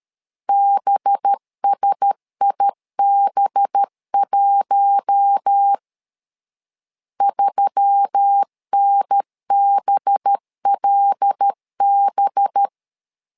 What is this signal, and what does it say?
Morse code, heard as a keyed tone, about 13 wpm, 800 Hz. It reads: BSIB1 3NBLB
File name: LEZIONI / 5 LBVC / 5 LB 12wpm 1906